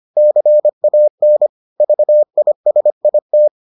Morse code, heard as a keyed tone, about 25 wpm, 600 Hz. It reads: CAN VISIT